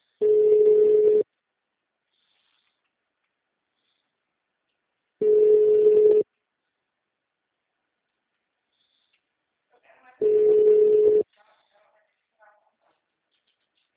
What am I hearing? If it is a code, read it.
Ringback tone (Europe, 425 Hz).
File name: out- -2034- - - .74